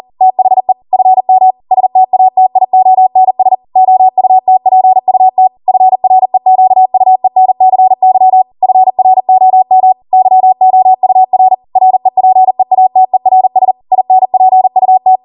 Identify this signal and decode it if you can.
Morse code, heard as a keyed tone, about 40 wpm, 750 Hz. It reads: THE FM STATIONS OUTPUT FREQUENCY FROM YOUR REPEATERS INPUT
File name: Morse / 40WPM